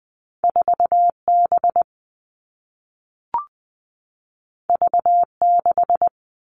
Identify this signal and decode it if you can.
Morse code, heard as a keyed tone, about 20 wpm, 700 Hz. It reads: 4B V6
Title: All Cut Numbers - 2-Digit 20wpm